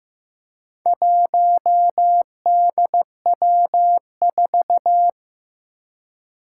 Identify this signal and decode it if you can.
Morse code, heard as a keyed tone, about 15 wpm, 700 Hz. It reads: 1DW4
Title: Four Challenging Letter-Number Combinations 15wpm